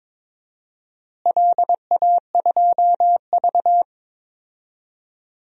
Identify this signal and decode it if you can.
Morse code, heard as a keyed tone, about 21 wpm, 700 Hz. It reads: LA2V
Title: DXCC-100 Call Signs - 4 Characters - Repeated 3x - 22wpm